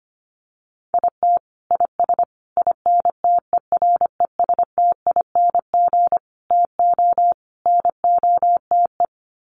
Morse code, heard as a keyed tone, about 25 wpm, 700 Hz. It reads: IT IS INTERESTING TO NOTE
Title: Sets of 5 Words Encore2 25wpm